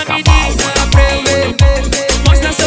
There is no code.